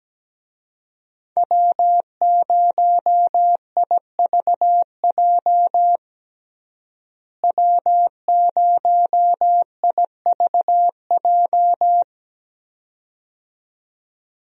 Morse code, heard as a keyed tone, about 17 wpm, 700 Hz. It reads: W0IVJ W0IVJ
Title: USA Call Signs - 5 Characters - Repeated 2x - 17wpm